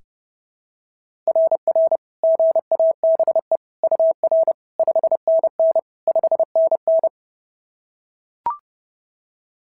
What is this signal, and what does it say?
Morse code, heard as a keyed tone, about 28 wpm, 650 Hz. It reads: RR GABE UR 5NN 5NN
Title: SOTA QSO Practice 30wpm